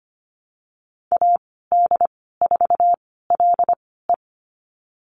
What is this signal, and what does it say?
Morse code, heard as a keyed tone, about 25 wpm, spaced as for 16 wpm, 700 Hz. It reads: AD4LE
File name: Lesson 15g - Number 5 - Call Signs 25wpm